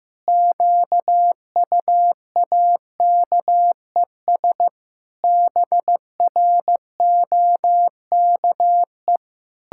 Morse code, heard as a keyed tone, about 15 wpm, 700 Hz. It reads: QUAKES BROKE
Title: Pangram Challenge v2 - 15wpm